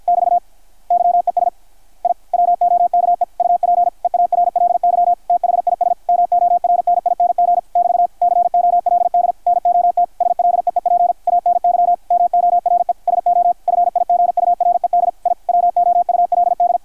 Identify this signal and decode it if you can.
Morse code, heard as a keyed tone, about 50 wpm, 660 Hz, with radio noise. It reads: <BT> YES I WOKE UP EARLY THIS MORNING <BT> COLD NOT SLEEP ANY MORE SO FIGURED I WOULD
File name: archived / CRSnet